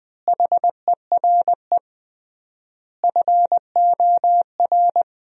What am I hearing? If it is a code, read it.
Morse code, heard as a keyed tone, about 20 wpm, 700 Hz. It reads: HERE FOR
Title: 3x Word-Spacing - Sentences from Top 200 Words - 20wpm